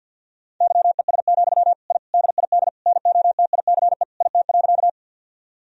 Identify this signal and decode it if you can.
Morse code, heard as a keyed tone, about 50 wpm, 700 Hz. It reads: YES, I DID NOTICE IT.
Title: Sentences from Top 1000 Words - Review Part 2 - 50wpm